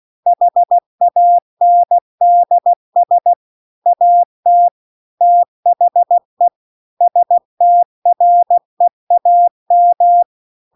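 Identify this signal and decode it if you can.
Morse code, heard as a keyed tone, about 16 wpm, 700 Hz. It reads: HANDS AT THE STREAM
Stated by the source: Morse practice text at 16 WPM.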